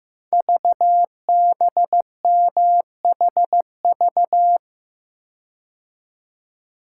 Morse code, heard as a keyed tone, about 15 wpm, 700 Hz. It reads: VBMHV
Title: Code Groups - 5 Letters - 15wpm